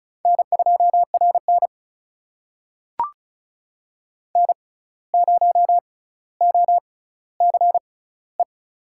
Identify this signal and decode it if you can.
Morse code, heard as a keyed tone, about 36 wpm, spaced as for 17 wpm, 700 Hz. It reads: N2RN N0OCE
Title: USA Call Signs - 6x Character Spacing - 35wpm